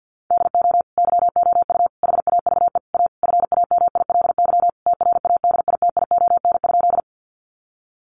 Morse code, heard as a keyed tone, about 50 wpm, 700 Hz. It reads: DO YOU HAVE A FAMILY TRADITION?
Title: Sentences from Top 1500 Words 50wpm